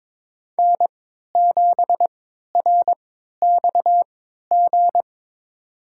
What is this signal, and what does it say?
Morse code, heard as a keyed tone, about 21 wpm, spaced as for 12 wpm, 700 Hz. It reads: N7RXG
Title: USA Call Signs - 3x Character Spacing - 22wpm